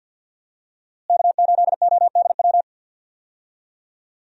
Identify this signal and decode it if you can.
Morse code, heard as a keyed tone, about 52 wpm, 700 Hz. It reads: K8ODW
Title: USA Call Signs - 5 Characters - Repeated 3x - 50wpm